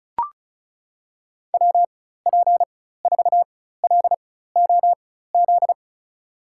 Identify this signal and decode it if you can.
Morse code, heard as a keyed tone, about 36 wpm, spaced as for 16 wpm, 700 Hz. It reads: WP4LOZ
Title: USA Call Signs - 6 Characters - 4x Character Spacing - 35wpm